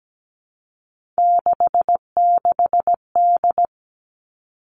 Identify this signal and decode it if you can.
Morse code, heard as a keyed tone, about 17 wpm, 700 Hz. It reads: 66D
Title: Three Challenging Letter-Number Combinations 17wpm